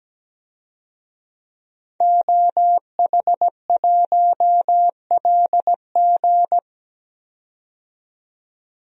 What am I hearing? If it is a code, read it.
Morse code, heard as a keyed tone, about 17 wpm, 700 Hz. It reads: OH1LG
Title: DXCC-100 Call Signs - 5 Characters - 17wpm